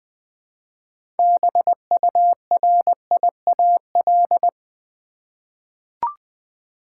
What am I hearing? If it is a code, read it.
Morse code, heard as a keyed tone, about 20 wpm, 700 Hz. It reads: BURIAL
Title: 6 Letter Words - First 3 Letters Given - 20wpm